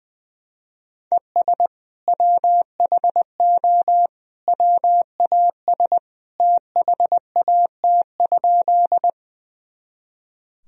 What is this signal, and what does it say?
Morse code, heard as a keyed tone, about 21 wpm, 700 Hz. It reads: ES WHO WAS THAT?
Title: Sentences from Top 100 Words and Abbreviations 20wpm